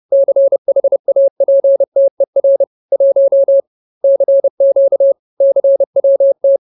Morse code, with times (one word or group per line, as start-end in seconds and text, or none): 0.1-2.6 CHAPTER
2.9-3.6 1
4.0-5.1 CQ
5.4-6.6 CWT